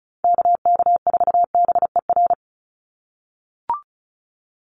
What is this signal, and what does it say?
Morse code, heard as a keyed tone, about 34 wpm, 700 Hz. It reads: KK4BER
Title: USA Call Signs 35wpm